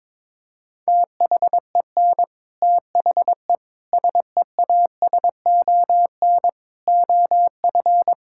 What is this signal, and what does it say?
Morse code, heard as a keyed tone, about 22 wpm, 700 Hz. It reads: THEN THE SEASON OF
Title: Sentences from Top 1600 Words 22wpm